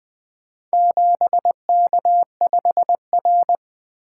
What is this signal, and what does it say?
Morse code, heard as a keyed tone, about 20 wpm, 700 Hz. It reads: 7K5R
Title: Four Letter-Number Combinations - 20wpm